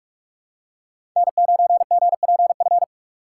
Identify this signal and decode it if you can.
Morse code, heard as a keyed tone, about 46 wpm, 700 Hz. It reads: N9GPF
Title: USA Call Signs - 4x Character Spacing - 45wpm